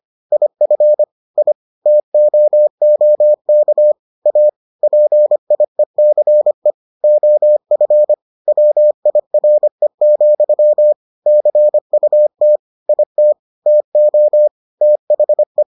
Morse code, heard as a keyed tone, about 25 wpm, 600 Hz. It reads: IF I TOOK A PIECE OF WIRE, CUT IT TO THE